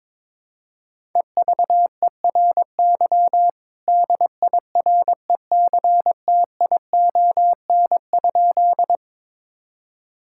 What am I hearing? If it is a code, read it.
Morse code, heard as a keyed tone, about 22 wpm, 700 Hz. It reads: EVERY DIRECTION?
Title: Sentences from Top 1000 Words - Review Part 5 - 22wpm